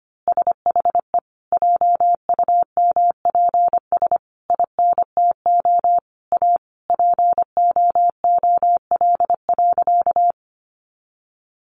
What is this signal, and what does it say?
Morse code, heard as a keyed tone, about 25 wpm, 700 Hz. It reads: SHE JUMPS INTO A POOL.
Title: Easy Phrases 25wpm